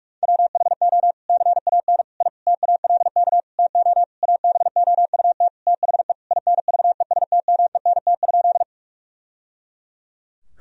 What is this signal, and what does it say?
Morse code, heard as a keyed tone, about 42 wpm, 700 Hz. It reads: WHO CAN I TALK TO ABOUT THE INVESTMENT?
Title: Sentences from Top 1000 Words - Review Part 4 - 45wpm